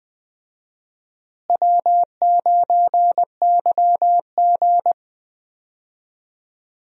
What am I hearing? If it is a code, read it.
Morse code, heard as a keyed tone, about 20 wpm, 700 Hz. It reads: W9YG
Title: USA Call Signs - 4 Characters - Repeated 2x - 20wpm